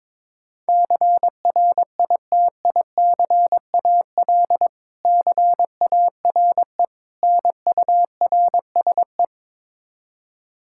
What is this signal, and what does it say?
Morse code, heard as a keyed tone, about 22 wpm, 700 Hz. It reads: CRITICAL CARE NURSE
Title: Practice QSO - Exchange 3 - More Info - 22wpm